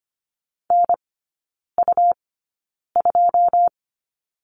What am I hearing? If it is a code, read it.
Morse code, heard as a keyed tone, about 25 wpm, spaced as for 9 wpm, 700 Hz. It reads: NU2
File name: Lesson 17e - Letter u - Three Characters 25wpm